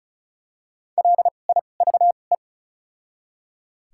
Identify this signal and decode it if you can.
Morse code, heard as a keyed tone, about 34 wpm, spaced as for 25 wpm, 700 Hz. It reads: LIVE
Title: 2x Character Spacing - Top 300 Words - 35wpm